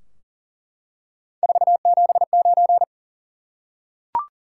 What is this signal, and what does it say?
Morse code, heard as a keyed tone, about 40 wpm, 700 Hz. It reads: 479
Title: Top 180 QSO Elements 40wpm